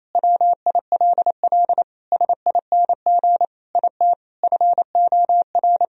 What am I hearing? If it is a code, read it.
Morse code, heard as a keyed tone, about 28 wpm, 700 Hz. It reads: WILL SING IT FOR
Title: Sentences from Top 1000 Words - Review Part 2 - 28wpm